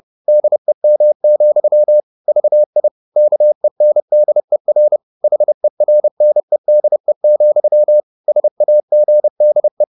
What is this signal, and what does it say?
Morse code, heard as a keyed tone, about 30 wpm, 600 Hz. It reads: DEM, VI KENDER HERNEDE, SAGDE
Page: Ferie 30 wpm